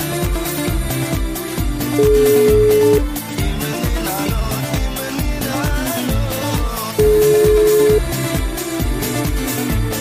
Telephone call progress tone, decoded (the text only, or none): ringback tone